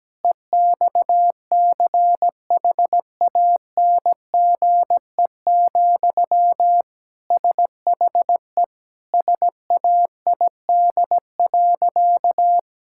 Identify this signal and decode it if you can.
Morse code, heard as a keyed tone, about 17 wpm, 700 Hz. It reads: EXCHANGE, SHE SAID.
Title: Sentences from Top 1000 Words - Review Part 6 - 17wpm